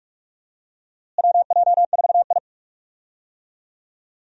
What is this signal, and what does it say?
Morse code, heard as a keyed tone, about 44 wpm, 700 Hz. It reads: WJ4I
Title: USA Call Signs - 4 Characters - Repeated 3x - 45wpm